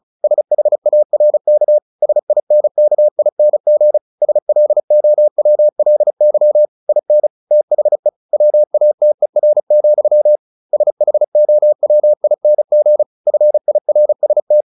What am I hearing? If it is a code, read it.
Morse code, heard as a keyed tone, about 34 wpm, 600 Hz. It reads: SHARK SINKING SLOWLY IN THE WATER, SHOWING FIRST
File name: the old man and the sea 2 35wpm